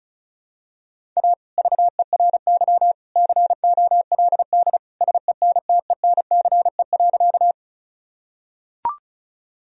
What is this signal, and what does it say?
Morse code, heard as a keyed tone, about 36 wpm, 700 Hz. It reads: A VERY COLD SENTENCE.
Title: Sentences from Top 1000 Words - Review Part 1 - 35wpm